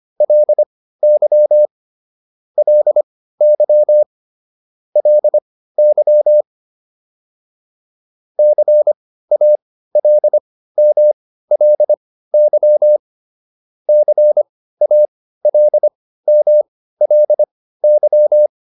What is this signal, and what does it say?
Morse code, heard as a keyed tone, about 25 wpm, spaced as for 15 wpm, 600 Hz. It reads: LY LY LY CALMLY CALMLY